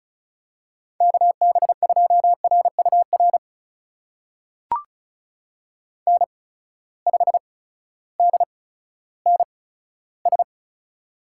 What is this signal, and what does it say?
Morse code, heard as a keyed tone, about 36 wpm, spaced as for 16 wpm, 700 Hz. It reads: KB2RUR N5DNS
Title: USA Call Signs - 8x Character Spacing - 35wpm